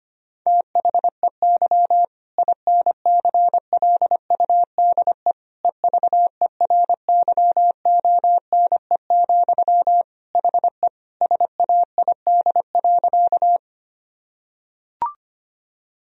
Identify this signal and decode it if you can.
Morse code, heard as a keyed tone, about 25 wpm, 700 Hz. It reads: THEY INCLUDE EVERYONE, HE SAID.
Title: Sentences from Top 1000 Words - Review Part 3 - 25wpm